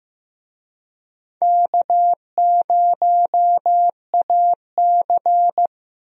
Morse code, heard as a keyed tone, about 15 wpm, 700 Hz. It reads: K0AC
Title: USA Call Signs - 4 Characters - Repeated 2x - 15wpm